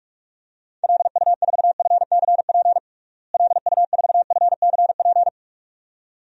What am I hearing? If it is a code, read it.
Morse code, heard as a keyed tone, about 46 wpm, 700 Hz. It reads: LU4FCP LU4FCP
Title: DXCC-100 Call Signs - 6 Characters - Repeated 2x - 45wpm